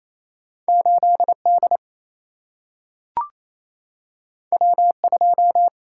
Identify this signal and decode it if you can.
Morse code, heard as a keyed tone, about 28 wpm, 700 Hz. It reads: 8 D W 2
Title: Single-Letter Single-Number 28wpm